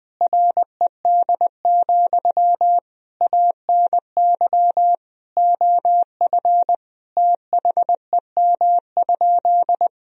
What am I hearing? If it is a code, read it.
Morse code, heard as a keyed tone, about 20 wpm, 700 Hz. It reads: RED, ANY OF THEM?
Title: Sentences from Top 1000 Words - Review Part 2 - 20wpm